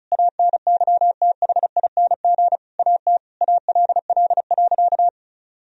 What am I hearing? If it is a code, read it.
Morse code, heard as a keyed tone, about 36 wpm, 700 Hz. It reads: ANYTHING AT ALL.
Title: Sentences from Top 1000 Words - Review Part 1 - 35wpm